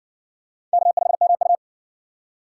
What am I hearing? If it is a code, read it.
Morse code, heard as a keyed tone, about 52 wpm, 700 Hz. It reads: K5MU